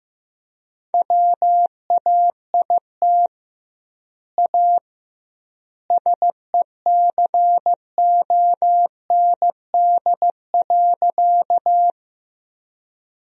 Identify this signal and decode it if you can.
Morse code, heard as a keyed tone, about 15 wpm, 700 Hz. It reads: WAIT A SECOND.
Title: 2x Word-Spacing - Sentences from Top 500 Words - 15wpm